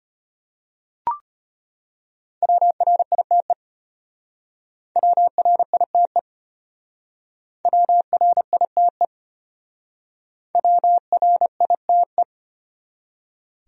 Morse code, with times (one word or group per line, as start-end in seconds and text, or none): 2.4-3.5 WRITE
5.0-6.2 WRITE
7.7-9.1 WRITE
10.6-12.2 WRITE